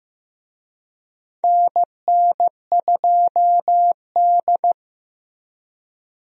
Morse code, heard as a keyed tone, about 15 wpm, 700 Hz. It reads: NN2D
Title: USA Call Signs - 4 Characters - Repeated 3x - 15wpm